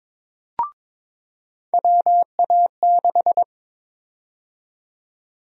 Morse code, heard as a keyed tone, about 22 wpm, 700 Hz. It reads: WA6
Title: USA Call Sign Prefixes - 22wpm